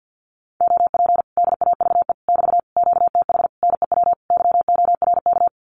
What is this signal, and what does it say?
Morse code, heard as a keyed tone, about 50 wpm, 700 Hz. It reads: OP DAVE <BT> QTH NEW YORK
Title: Common QSO Phrases Encore 50wpm